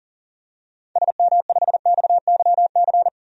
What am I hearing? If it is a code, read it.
Morse code, heard as a keyed tone, about 39 wpm, 700 Hz. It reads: SM5XYC
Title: DXCC-100 Call Signs - 6 Characters - 40wpm